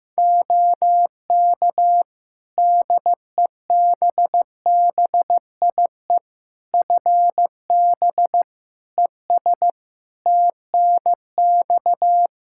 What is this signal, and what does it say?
Morse code, heard as a keyed tone, about 15 wpm, 700 Hz. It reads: OK DEBBIE FB ES TNX
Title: Practice QSO - Exchange 2 - Info - 15wpm